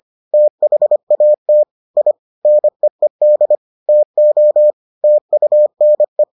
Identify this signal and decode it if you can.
Morse code, heard as a keyed tone, about 25 wpm, 600 Hz. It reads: THAT I NEED TO TUNE